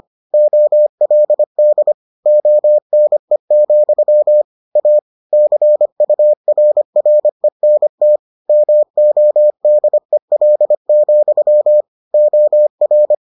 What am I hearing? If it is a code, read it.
Morse code, heard as a keyed tone, about 25 wpm, 600 Hz. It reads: OLD ONE, A CURRENT MODEL, OR